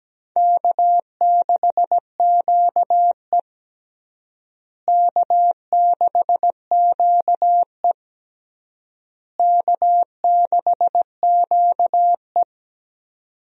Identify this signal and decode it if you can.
Morse code, heard as a keyed tone, about 17 wpm, 700 Hz. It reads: K6QE K6QE K6QE
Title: USA Call Signs - 4 Characters - Repeated 3x - 17wpm